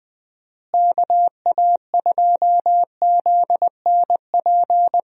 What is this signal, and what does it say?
Morse code, heard as a keyed tone, about 20 wpm, 700 Hz. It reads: KA2ZNP
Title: USA Call Signs - 6 Characters - Repeated 2x - 20wpm